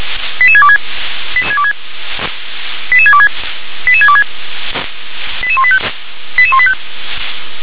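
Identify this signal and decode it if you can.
Five-tone selective call (ZVEI), heard as quick runs of five tones, tones of about 70 ms. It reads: 88626 88626 88626 88185 88185
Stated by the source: Five-tone ZVEI selective call 88625, received via SDR.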